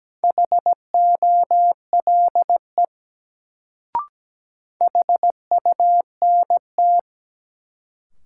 Morse code, heard as a keyed tone, about 17 wpm, 700 Hz. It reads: HOLE HUNT
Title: Sentences from Top 1600 Words - New Word Drill Alt - 17wpm